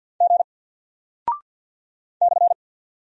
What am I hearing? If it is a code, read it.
Morse code, heard as a keyed tone, about 48 wpm, 700 Hz. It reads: G /
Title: Single Character (Letter:Number:Punctuation:Prosign) 50wpm